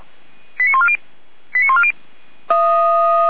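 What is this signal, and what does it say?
Five-tone selective call (ZVEI), heard as quick runs of five tones, tones of about 70 ms. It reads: 89140 89140
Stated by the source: Five-tone ZVEI selective call received by SDR.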